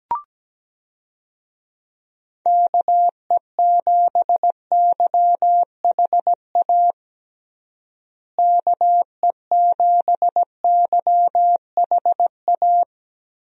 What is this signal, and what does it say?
Morse code, heard as a keyed tone, about 17 wpm, 700 Hz. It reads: KE7YHA KE7YHA
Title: USA Call Signs - 6 Characters - Repeated 3x - 17wpm